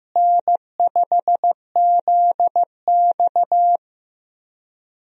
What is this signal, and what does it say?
Morse code, heard as a keyed tone, about 15 wpm, 700 Hz. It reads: N5ZX
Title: USA Call Signs - 4 Characters - Repeated 3x - 15wpm